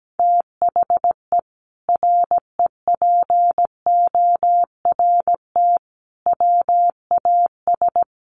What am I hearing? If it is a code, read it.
Morse code, heard as a keyed tone, about 17 wpm, 700 Hz. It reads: THE REPORT WAS
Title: Sentences from Top 900 Words 17wpm